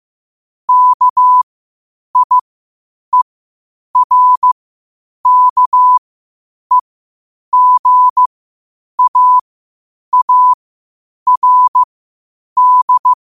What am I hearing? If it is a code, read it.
Morse code, heard as a keyed tone, about 15 wpm, spaced as for 8 wpm, 1000 Hz. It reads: KIERKEGAARD
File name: qotdfiles / 8wpm